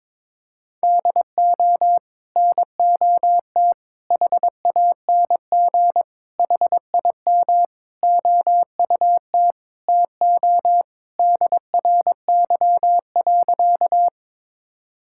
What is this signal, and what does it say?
Morse code, heard as a keyed tone, about 22 wpm, 700 Hz. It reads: DO NOT HANG HIM OUT TO DRY.
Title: Sentences from Top 1600 Words 22wpm